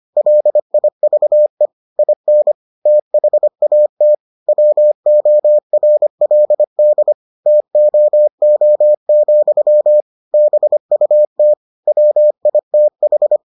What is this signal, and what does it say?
Morse code, heard as a keyed tone, about 25 wpm, 600 Hz. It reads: LIVE IN THAT WORLD TOO, BUT WITH